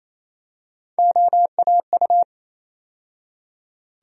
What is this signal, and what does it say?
Morse code, heard as a keyed tone, about 27 wpm, 700 Hz. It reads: OAU
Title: USA Call Sign Postfixes - 3 Letters 28wpm